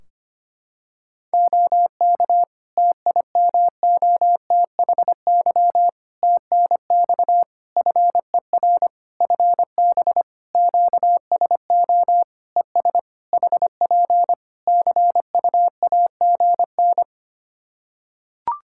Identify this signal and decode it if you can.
Morse code, heard as a keyed tone, about 25 wpm, 700 Hz. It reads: OK TIMOTHY TNX FER FB QSO ES HP CUAGN
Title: K1USN SST QSO Practice 25wpm